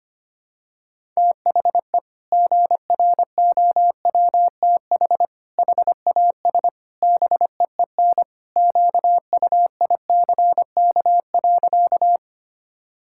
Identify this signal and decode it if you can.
Morse code, heard as a keyed tone, about 25 wpm, 700 Hz. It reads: THE GROWTH HAS BEEN QUICK.
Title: Sentences from Top 1000 Words - Review Part 5 - 25wpm